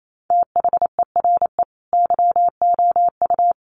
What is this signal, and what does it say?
Morse code, heard as a keyed tone, about 28 wpm, 700 Hz. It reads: THERE YOU
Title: Sentences from Top 100 Words 28wpm